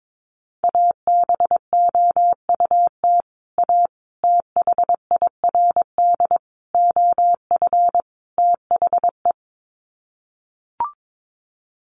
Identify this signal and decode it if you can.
Morse code, heard as a keyed tone, about 22 wpm, 700 Hz. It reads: ABOUT A THIRD OF THE
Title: Sets of 5 Words Encore2 22wpm